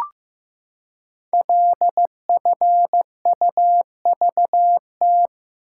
Morse code, heard as a keyed tone, about 15 wpm, 700 Hz. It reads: LFUVT
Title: Code Groups - 5 Letters - 15wpm